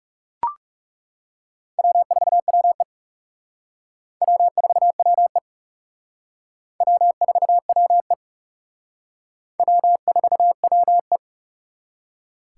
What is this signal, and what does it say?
Morse code, heard as a keyed tone, about 36 wpm, 700 Hz. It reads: W4WE W4WE W4WE W4WE
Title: Speed Racing - USA 1x2 Call Signs 30wpm